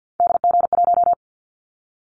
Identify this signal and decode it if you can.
Morse code, heard as a keyed tone, about 50 wpm, 700 Hz. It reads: DG1
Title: Callsign Prefixes for Top 25 DXCC - 50wpm